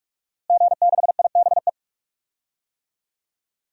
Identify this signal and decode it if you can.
Morse code, heard as a keyed tone, about 46 wpm, 700 Hz. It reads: G6IBE
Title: DXCC-100 Call Signs - 5 Characters - Repeated 2x - 45wpm